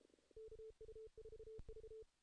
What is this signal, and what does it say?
Morse code, heard as a keyed tone, about 32 wpm, 440 Hz. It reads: KU4V
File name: 2016 JAN CW NAQP / 2016 Jan naqp k5ot - 0-8